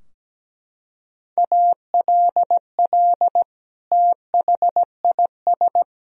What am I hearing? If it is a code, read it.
Morse code, heard as a keyed tone, about 17 wpm, 700 Hz. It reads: ALL THIS
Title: Sentences from Top 1000 Words - Review Part 4 - 17wpm